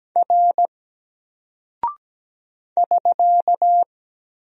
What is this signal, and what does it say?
Morse code, heard as a keyed tone, about 17 wpm, 700 Hz. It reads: R <SK>
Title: Single Character (Letter:Number:Punctuation:Prosign) 17wpm